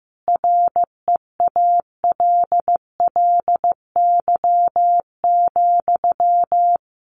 Morse code, heard as a keyed tone, about 15 wpm, 700 Hz. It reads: REALLY,
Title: Sentences from Top 1000 Words 15wpm